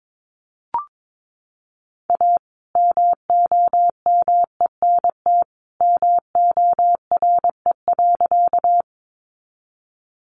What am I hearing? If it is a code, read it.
Morse code, heard as a keyed tone, about 22 wpm, 700 Hz. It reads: A MOMENT MORE.
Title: Sentences from Top 1000 Words 22wpm